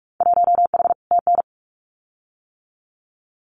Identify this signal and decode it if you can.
Morse code, heard as a keyed tone, about 44 wpm, 700 Hz. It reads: 1H TN
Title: Winter Field Day Exchanges 45wpm